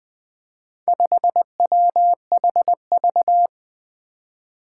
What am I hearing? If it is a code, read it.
Morse code, heard as a keyed tone, about 20 wpm, 700 Hz. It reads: 5WHV
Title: Four Challenging Letter-Number Combinations 20wpm